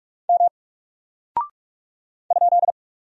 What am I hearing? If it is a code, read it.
Morse code, heard as a keyed tone, about 46 wpm, 700 Hz. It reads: M ?
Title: Single Character (Letter:Number:Punctuation:Prosign) 45wpm